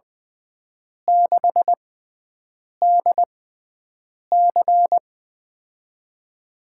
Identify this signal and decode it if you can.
Morse code, heard as a keyed tone, about 20 wpm, 700 Hz. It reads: 6 D C
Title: USA Call Signs - 6x Character Spacing - 20wpm